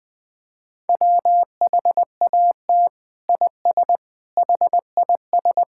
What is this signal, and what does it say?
Morse code, heard as a keyed tone, about 20 wpm, 700 Hz. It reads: WHAT IS HIS
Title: Sentences from Top 1000 Words - Review Part 5 - 20wpm